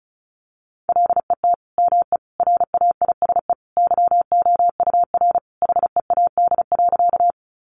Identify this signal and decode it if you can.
Morse code, heard as a keyed tone, about 34 wpm, 700 Hz. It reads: LET ME RAISE YOUR HEAD.
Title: Sentences from Top 1400 Words 35wpm